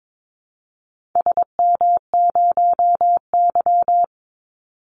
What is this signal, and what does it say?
Morse code, heard as a keyed tone, about 22 wpm, 700 Hz. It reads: SM0Y
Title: Call Signs - 22wpm